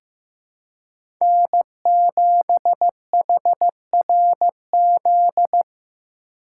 Morse code, heard as a keyed tone, about 15 wpm, 700 Hz. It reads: N7HRZ
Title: USA Call Signs - 5 Characters - 2x Character Spacing - 15wpm